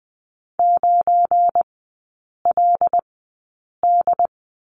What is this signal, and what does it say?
Morse code, heard as a keyed tone, about 20 wpm, spaced as for 8 wpm, 700 Hz. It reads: 9LD
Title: 205 Lesson 33e - Letter j - Three Characters 20wpm